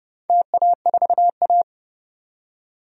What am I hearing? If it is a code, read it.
Morse code, heard as a keyed tone, about 30 wpm, 700 Hz. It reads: TA4A
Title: DXCC-100 Call Signs - 4 Characters - 30wpm